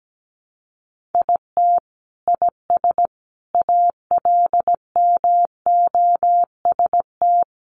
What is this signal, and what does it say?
Morse code, heard as a keyed tone, about 17 wpm, 700 Hz. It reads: IT IS ALMOST
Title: Sentences from Top 1200 Words 17wpm